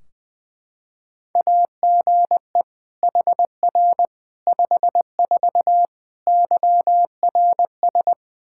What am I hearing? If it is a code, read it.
Morse code, heard as a keyed tone, about 20 wpm, 700 Hz. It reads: AGE HR 54 YRS
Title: Practice QSO - Exchange 3 - More Info - 20wpm